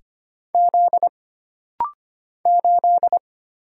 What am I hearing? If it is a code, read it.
Morse code, heard as a keyed tone, about 25 wpm, 700 Hz. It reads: Z 8
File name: Single Character (Letter:Number:Punctuation:Prosign) - Rapid-Fire - 25wpm